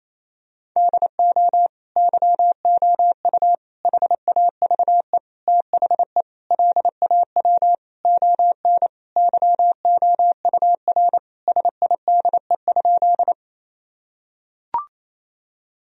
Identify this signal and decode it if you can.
Morse code, heard as a keyed tone, about 28 wpm, 700 Hz. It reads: DO YOU HAVE THE LAW ON YOUR SIDE?
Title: Sentences from Top 1000 Words - Review Part 3 - 28wpm